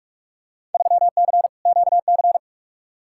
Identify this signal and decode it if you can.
Morse code, heard as a keyed tone, about 46 wpm, 700 Hz. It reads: 3C QC
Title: Field Day Exchanges Encore - 45wpm